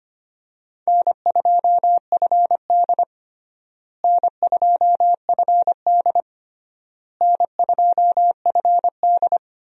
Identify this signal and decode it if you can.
Morse code, heard as a keyed tone, about 25 wpm, 700 Hz. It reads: N2FD N2FD N2FD
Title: USA Call Signs - 4 Characters - Repeated 3x - 25wpm